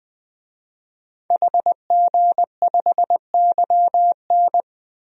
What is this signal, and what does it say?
Morse code, heard as a keyed tone, about 20 wpm, 700 Hz. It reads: HG5YN
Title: DXCC-100 Call Signs - 5 Characters - 20wpm